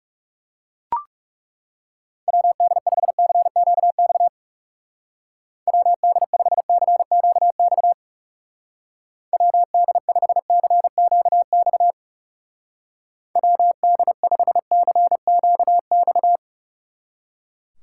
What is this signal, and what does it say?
Morse code, heard as a keyed tone, about 39 wpm, 700 Hz. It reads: WD5CQX WD5CQX WD5CQX WD5CQX